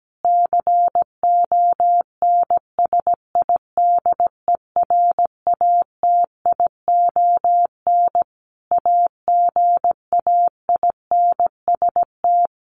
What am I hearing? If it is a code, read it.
Morse code, heard as a keyed tone, about 17 wpm, 700 Hz. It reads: CONSIDERATION AGAINST
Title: Sentences from Top 1500 Words 17wpm